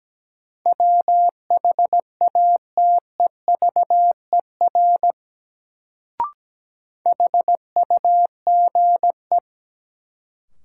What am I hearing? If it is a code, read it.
Morse code, heard as a keyed tone, about 17 wpm, 700 Hz. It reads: WHATEVER HUGE
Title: Sentences from Top 1100 Words - New Word Drill - 17wpm